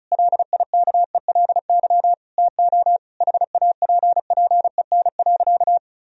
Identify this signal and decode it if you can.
Morse code, heard as a keyed tone, about 34 wpm, 700 Hz. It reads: LIKELY TO HAPPEN.
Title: Sentences from Top 1000 Words - Review Part 4 - 35wpm